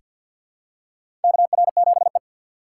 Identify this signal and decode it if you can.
Morse code, heard as a keyed tone, about 49 wpm, 700 Hz. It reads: KR7E
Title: USA Call Signs - 4 Characters - 6x Character Spacing - 50wpm